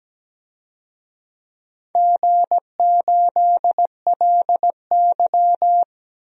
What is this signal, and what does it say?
Morse code, heard as a keyed tone, about 17 wpm, 700 Hz. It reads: G8LY
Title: DXCC-100 Call Signs - 4 Characters - Repeated 3x - 17wpm